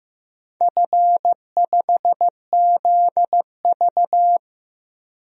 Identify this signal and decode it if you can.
Morse code, heard as a keyed tone, about 15 wpm, 700 Hz. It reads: F5ZV
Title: DXCC-100 Call Signs - 4 Characters - Repeated 2x - 15wpm